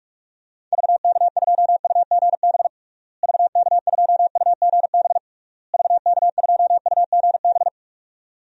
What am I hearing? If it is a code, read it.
Morse code, heard as a keyed tone, about 46 wpm, 700 Hz. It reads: VK2UGB VK2UGB VK2UGB
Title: DXCC-100 Call Signs - 6 Characters - Repeated 3x - 45wpm